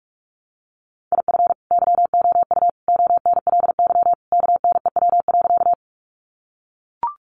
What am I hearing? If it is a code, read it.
Morse code, heard as a keyed tone, about 44 wpm, 700 Hz. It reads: IF YOU ONLY KNEW.
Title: Sentences from Top 400 Words 45wpm